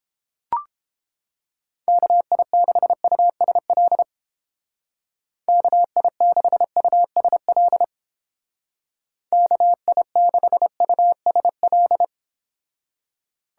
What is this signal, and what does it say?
Morse code, heard as a keyed tone, about 31 wpm, 700 Hz. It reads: KI6USL KI6USL KI6USL